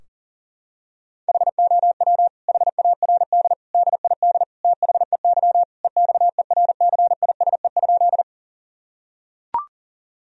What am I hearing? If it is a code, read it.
Morse code, heard as a keyed tone, about 40 wpm, 700 Hz. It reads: HOW HARD DID THEY EXERCISE?
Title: Sentences from Top 1000 Words - Review Part 4 - 40wpm